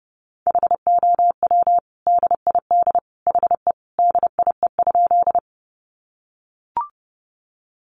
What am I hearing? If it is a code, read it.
Morse code, heard as a keyed tone, about 30 wpm, 700 Hz. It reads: HOW DID HE DIE?
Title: Sentences from Top 1300 Words 30wpm